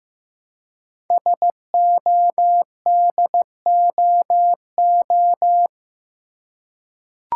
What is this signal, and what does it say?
Morse code, heard as a keyed tone, about 15 wpm, 700 Hz. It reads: SODOO
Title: Code Groups - 5 Letters - 15wpm